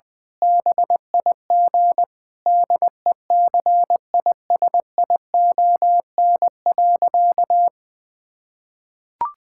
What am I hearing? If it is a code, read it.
Morse code, heard as a keyed tone, about 21 wpm, 700 Hz. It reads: BIG DECISION.
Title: Difficult Sentences with Hints - 20wpm